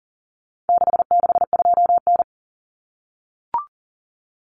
Morse code, heard as a keyed tone, about 40 wpm, 700 Hz. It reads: 662N
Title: Common Cut Numbers - 4-Digit 40wpm